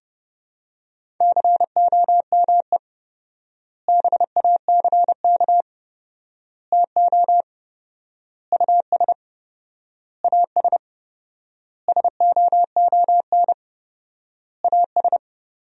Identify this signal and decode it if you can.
Morse code, heard as a keyed tone, about 30 wpm, 700 Hz. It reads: COME BACK TO US AS SOON AS
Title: 4x Word-Spacing - Sentences from Top 500 Words - 30wpm